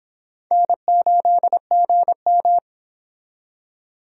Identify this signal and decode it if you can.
Morse code, heard as a keyed tone, about 32 wpm, 700 Hz. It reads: N8GM